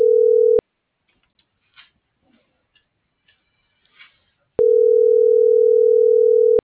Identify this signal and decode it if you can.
Ringback tone (North America), 440 + 480 Hz.